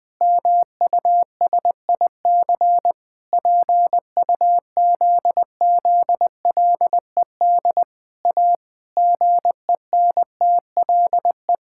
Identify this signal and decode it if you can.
Morse code, heard as a keyed tone, about 20 wpm, 700 Hz. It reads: MUSIC PUZZLED A GENTLE
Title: Pangram Challenge v2 - 20wpm